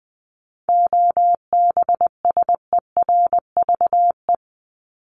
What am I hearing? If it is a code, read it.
Morse code, heard as a keyed tone, about 20 wpm, 700 Hz. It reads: OBSERVE
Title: ED Suffix - Top 100 Words with Base Words 20wpm